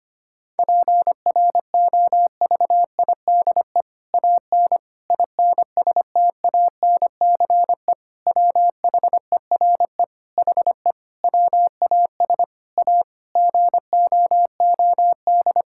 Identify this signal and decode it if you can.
Morse code, heard as a keyed tone, about 25 wpm, 700 Hz. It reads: PROVIDE AN INSTANCE WHERE HE WAS A GOOD
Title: Sentences from Top 1000 Words - Review Part 5 - 25wpm